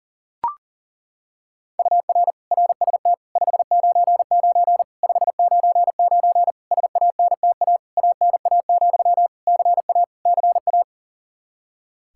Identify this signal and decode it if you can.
Morse code, heard as a keyed tone, about 40 wpm, 700 Hz. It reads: UR RST 599 599 SANTA ANA, CA CA
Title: Practice QSO - Exchange 1 - Beginning - 40wpm